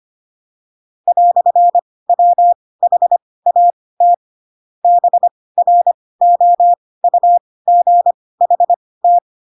Morse code, heard as a keyed tone, about 25 wpm, spaced as for 18 wpm, 700 Hz. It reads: "WHAT BROUGHT
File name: Pinocchio / Pinocchio ch2 12 25 ik1hge